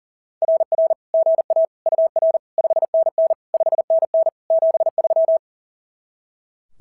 Morse code, heard as a keyed tone, about 40 wpm, 660 Hz. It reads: RR GA UR 5NN 5NN 73
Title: SOTA QSO Practice 40wpm